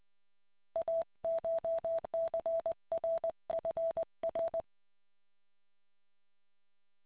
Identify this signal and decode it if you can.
Morse code, heard as a keyed tone, about 20 wpm, 660 Hz. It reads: ACHCRFR